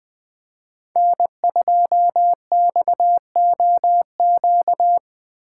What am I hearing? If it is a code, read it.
Morse code, heard as a keyed tone, about 20 wpm, 700 Hz. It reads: N2XOQ
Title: Speed Racing - USA Call Signs 17wpm